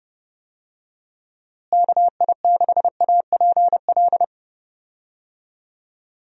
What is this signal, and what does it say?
Morse code, heard as a keyed tone, about 30 wpm, 700 Hz. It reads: KI6APL
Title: USA Call Signs - 6 Characters - Repeated 3x - 30wpm